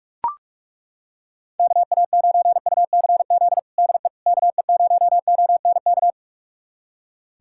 Morse code, heard as a keyed tone, about 42 wpm, 700 Hz. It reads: KA9UCZ DE KE0ONK
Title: USA Callsign DE Callsign - 6 Characters - 45wpm